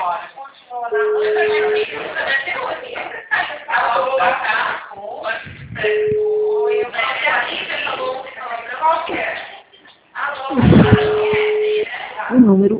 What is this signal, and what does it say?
Ringback tone (Europe, 425 Hz), with speech.